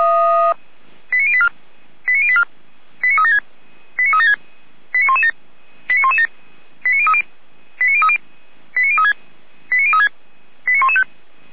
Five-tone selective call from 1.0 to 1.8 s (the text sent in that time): 89084